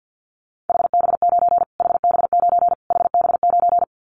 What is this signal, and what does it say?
Morse code, heard as a keyed tone, about 50 wpm, 700 Hz. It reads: HB9 HB9 HB9
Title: Callsign Prefixes for Top 25 DXCC - Repeated 3x - 50wpm